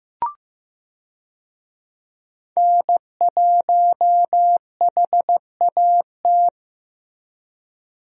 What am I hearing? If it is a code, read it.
Morse code, heard as a keyed tone, about 15 wpm, 700 Hz. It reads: N1HAT
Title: USA Call Signs - 5 Characters - Repeated 2x - 15wpm